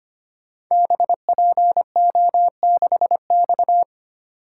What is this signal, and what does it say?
Morse code, heard as a keyed tone, about 25 wpm, 700 Hz. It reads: BPO6X
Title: Code Groups - 5 Letter:Numbers - 25wpm